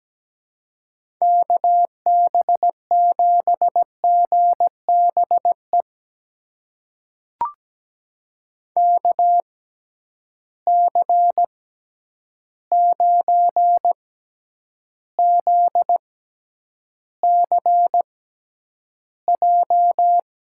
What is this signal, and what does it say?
Morse code, heard as a keyed tone, about 17 wpm, spaced as for 9 wpm, 700 Hz. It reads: KB7GBE KC9ZCJ
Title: USA Call Signs - 6 Characters - 6x Character Spacing - 17wpm